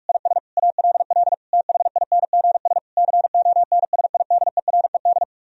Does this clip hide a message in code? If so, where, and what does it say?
Yes: Morse code from 0.1 to 5.2 s, heard as a keyed tone, about 49 wpm, 700 Hz. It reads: IS ALL THINGS CONSIDERED